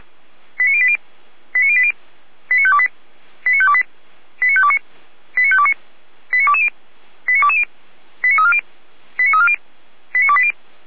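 Five-tone selective call (ZVEI), heard as quick runs of five tones, tones of about 70 ms. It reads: 89080 89080 89528 89528 89529 89529 89220 89220 89340 89340 89280
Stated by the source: ZVEI five-tone selective call received on SDR, from audio.f85360.feuerwehr-orlamuende.org fire brigade server.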